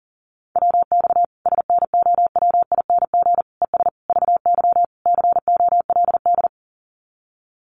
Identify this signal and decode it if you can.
Morse code, heard as a keyed tone, about 37 wpm, 700 Hz. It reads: WX SNOWING ES VY COLD
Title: Common QSO Phrases 40wpm